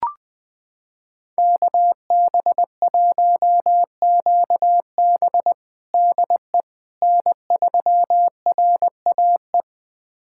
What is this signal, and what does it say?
Morse code, heard as a keyed tone, about 21 wpm, 700 Hz. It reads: KB1QB DE N3RAE
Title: USA Callsign DE Callsign - 5 Characters - 20wpm